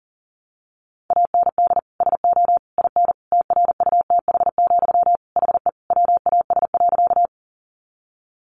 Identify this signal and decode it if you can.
Morse code, heard as a keyed tone, about 40 wpm, 700 Hz. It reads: AND SO IN TRUTH, HE WAS.
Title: Sentences from Top 800 Words 40wpm